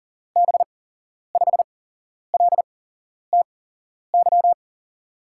Morse code, heard as a keyed tone, about 40 wpm, spaced as for 11 wpm, 700 Hz. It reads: B5LTY
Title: USA Call Signs - 6 Characters - 8x Character Spacing - 40wpm